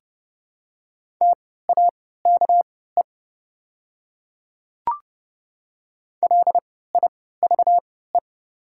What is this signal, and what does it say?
Morse code, heard as a keyed tone, about 30 wpm, spaced as for 17 wpm, 700 Hz. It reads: TAKE LIVE
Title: 3x Character Spacing - Top 300 Words - 30wpm